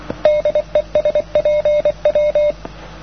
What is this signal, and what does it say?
Morse code, heard as a keyed tone, about 28 wpm, 590 Hz. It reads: DE SPW
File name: europe / pol / pol-252-spw-cw-only-andystumpf